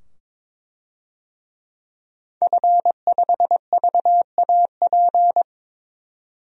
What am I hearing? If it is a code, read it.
Morse code, heard as a keyed tone, about 22 wpm, 700 Hz. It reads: F5VAP
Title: DXCC-100 Call Signs - 5 Characters - 22wpm